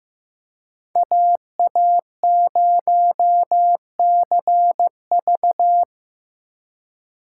Morse code, heard as a keyed tone, about 17 wpm, 700 Hz. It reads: AA0CV
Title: Speed Racing - USA 2x2 Call Signs 15wpm